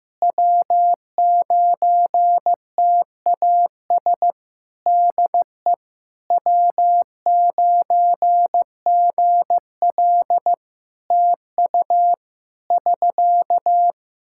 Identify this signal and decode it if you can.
Morse code, heard as a keyed tone, about 15 wpm, 700 Hz. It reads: W9TAS DE W9GL TU <SK>
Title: Practice QSO - Exchange 4 - Ending - 15wpm